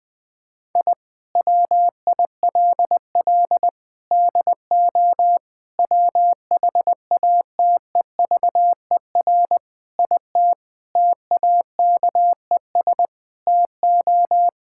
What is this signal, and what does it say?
Morse code, heard as a keyed tone, about 20 wpm, 700 Hz. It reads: I WILL DO WHATEVER IT TAKES TO
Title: Sentences with 6-Word Phrases - Spoken Beginning - 20wpm